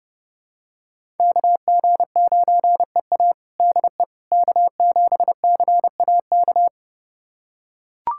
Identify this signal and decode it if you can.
Morse code, heard as a keyed tone, about 32 wpm, 700 Hz. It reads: KG9EA DE K7CAK
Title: USA Callsign DE Callsign - 5 Characters - 30wpm